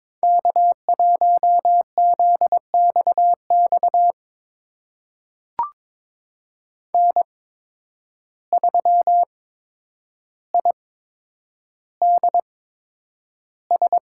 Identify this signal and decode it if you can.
Morse code, heard as a keyed tone, about 21 wpm, spaced as for 9 wpm, 700 Hz. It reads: K1ZXX N3IDS
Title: USA Call Signs - 5 Characters - 8x Character Spacing - 22wpm